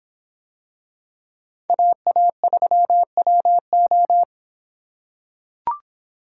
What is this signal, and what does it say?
Morse code, heard as a keyed tone, about 26 wpm, 700 Hz. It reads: AA3WO